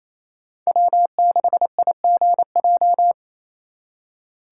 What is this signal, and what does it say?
Morse code, heard as a keyed tone, about 34 wpm, 700 Hz. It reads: W6IGJ
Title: Speed Racing - USA 1x3 Call Signs 28wpm